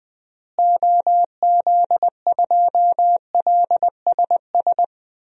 Morse code, heard as a keyed tone, about 20 wpm, 700 Hz. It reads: OZ2LSS
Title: DXCC-100 Call Signs - 6 Characters - Repeated 3x - 20wpm